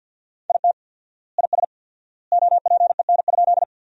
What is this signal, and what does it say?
Morse code, heard as a keyed tone, about 50 wpm, 700 Hz. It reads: IT IS OPEN?
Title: 4x Word-Spacing - Sentences from Top 300 Words - 50wpm